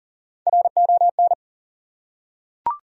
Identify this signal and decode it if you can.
Morse code, heard as a keyed tone, about 40 wpm, 700 Hz. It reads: RON
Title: USA Common Names - Max 4 Letters - 40wpm